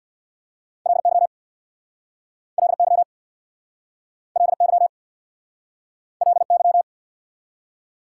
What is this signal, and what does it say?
Morse code, heard as a keyed tone, about 69 wpm, 690 Hz. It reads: LY LY LY LY